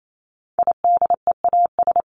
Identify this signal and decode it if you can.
Morse code, heard as a keyed tone, about 28 wpm, 700 Hz. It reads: IDEAS
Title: ES:S Suffix - Top 100 Words with Base Words 28wpm